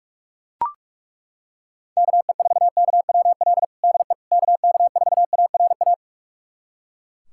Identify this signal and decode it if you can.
Morse code, heard as a keyed tone, about 41 wpm, 700 Hz. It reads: KE4KWL DE KK4ARA
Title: USA Callsign DE Callsign - 6 Characters - 45wpm